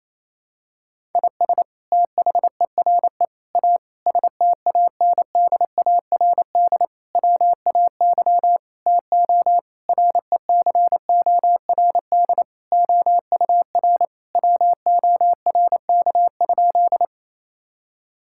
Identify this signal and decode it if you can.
Morse code, heard as a keyed tone, about 28 wpm, 700 Hz. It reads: IS THERE A STANDARD WAY TO RECORD OUR WORK?
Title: Sentences from Top 1000 Words - Review Part 6 - 28wpm